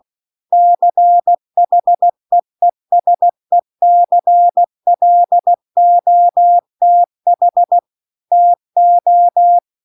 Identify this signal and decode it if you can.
Morse code, heard as a keyed tone, about 16 wpm, 700 Hz. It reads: CHEESECLOTH TO